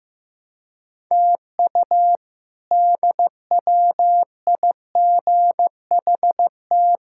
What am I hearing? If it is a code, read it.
Morse code, heard as a keyed tone, about 16 wpm, 700 Hz. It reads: TU DWIGHT
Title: K1USN SST QSO Practice - 15wpm